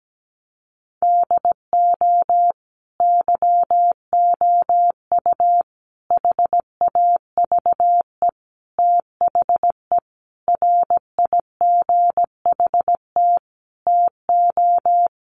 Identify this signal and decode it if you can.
Morse code, heard as a keyed tone, about 17 wpm, 700 Hz. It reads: DO YOU HAVE THE RIGHT TO
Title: Sentences from Top 1400 Words 17wpm